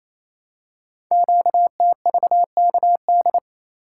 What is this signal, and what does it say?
Morse code, heard as a keyed tone, about 28 wpm, 700 Hz. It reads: QTVKD
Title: Code Groups - 5 Letter:Numbers - 28wpm